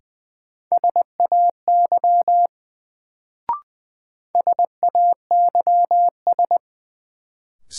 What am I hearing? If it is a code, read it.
Morse code, heard as a keyed tone, about 20 wpm, 700 Hz. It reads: SAY SAYS
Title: Sets of 2 Words - New Word Drill - 20wpm